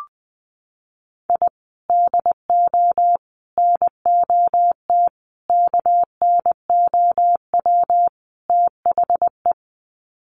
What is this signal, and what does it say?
Morse code, heard as a keyed tone, about 20 wpm, 700 Hz. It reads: I DO NOT KNOW THE
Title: Sets of 5 Words 20wpm